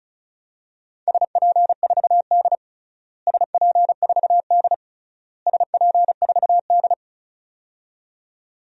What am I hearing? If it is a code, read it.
Morse code, heard as a keyed tone, about 34 wpm, 700 Hz. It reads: SP4D SP4D SP4D
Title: DXCC-100 Call Signs - 4 Characters - Repeated 3x - 35wpm